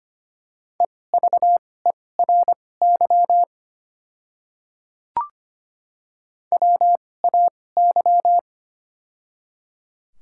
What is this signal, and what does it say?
Morse code, heard as a keyed tone, about 25 wpm, spaced as for 18 wpm, 690 Hz. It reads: EVERY WAY
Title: 2x Character Spacing - Top 300 Words - 25wpm